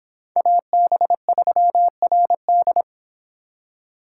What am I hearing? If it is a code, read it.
Morse code, heard as a keyed tone, about 26 wpm, 700 Hz. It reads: AB3RD